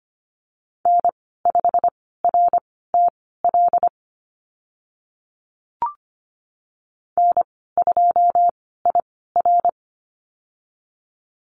Morse code, heard as a keyed tone, about 25 wpm, spaced as for 16 wpm, 700 Hz. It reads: N5RTL N2IR
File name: Lesson 15g - Number 5 - Call Signs 25wpm